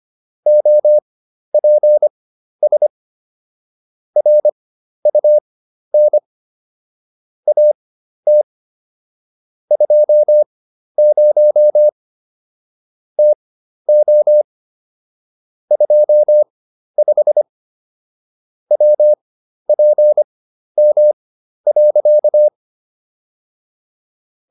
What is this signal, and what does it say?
Morse code, heard as a keyed tone, about 25 wpm, spaced as for 12 wpm, 600 Hz. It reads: OPS RUN AT 20 TO 25 WPM.